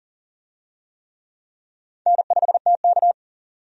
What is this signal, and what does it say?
Morse code, heard as a keyed tone, about 40 wpm, 700 Hz. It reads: N5TK
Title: USA Call Signs - 4 Characters - 40wpm